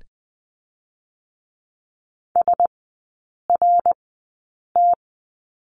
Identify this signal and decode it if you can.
Morse code, heard as a keyed tone, about 20 wpm, spaced as for 8 wpm, 700 Hz. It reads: SRT
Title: 107 Lesson 18e - Letter m - Three Characters 20wpm